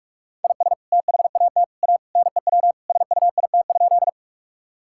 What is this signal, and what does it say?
Morse code, heard as a keyed tone, about 46 wpm, 700 Hz. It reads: IS THAT A NEW SUIT?
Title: Sentences from Top 1600 Words 45wpm